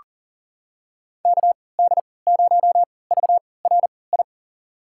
Morse code, heard as a keyed tone, about 40 wpm, spaced as for 23 wpm, 700 Hz. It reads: KD0VRI
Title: USA Call Signs - 6 Characters - 3x Character Spacing - 40wpm